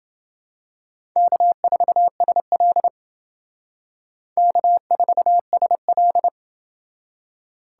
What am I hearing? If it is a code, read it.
Morse code, heard as a keyed tone, about 29 wpm, 700 Hz. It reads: K4SL K4SL